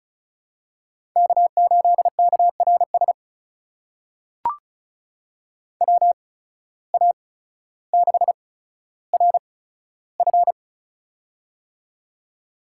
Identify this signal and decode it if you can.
Morse code, heard as a keyed tone, about 34 wpm, spaced as for 15 wpm, 700 Hz. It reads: K8KRS WA6RF
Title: USA Call Signs - 5 Characters - 8x Character Spacing - 35wpm